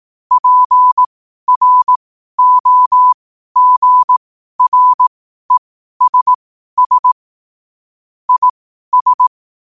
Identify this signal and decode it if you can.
Morse code, heard as a keyed tone, about 18 wpm, spaced as for 12 wpm, 1000 Hz. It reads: PROGRESS IS
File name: qotdfiles / 12wpm